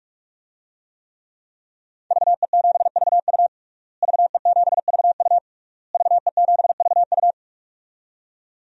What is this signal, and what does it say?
Morse code, heard as a keyed tone, about 41 wpm, 700 Hz. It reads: VE7VU VE7VU VE7VU
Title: DXCC-100 Call Signs - 5 Characters - Repeated 3x - 45wpm